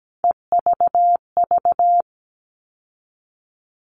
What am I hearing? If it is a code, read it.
Morse code, heard as a keyed tone, about 17 wpm, 700 Hz. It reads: EVV
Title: All Cut Numbers - 3-Digit 17wpm